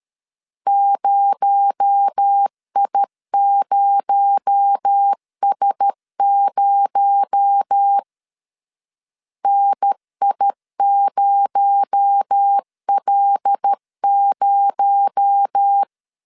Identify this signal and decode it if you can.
Morse code, heard as a keyed tone, about 13 wpm, 800 Hz. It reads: 0I0S0 NI0L0
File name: LEZIONI / 6 PKX0 / 6 PKX0 12wpm 2231